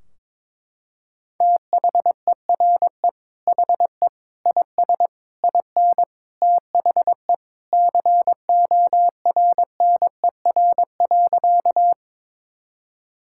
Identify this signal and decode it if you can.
Morse code, heard as a keyed tone, about 22 wpm, 700 Hz. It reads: THERE HE IS IN THE CORNER.
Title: Sentences from Top 1600 Words 22wpm